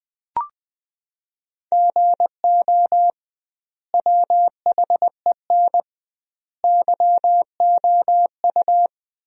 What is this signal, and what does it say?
Morse code, heard as a keyed tone, about 20 wpm, 700 Hz. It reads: GO WHEN YOU
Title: 2x Word-Spacing - Sentences from Top 500 Words - 20wpm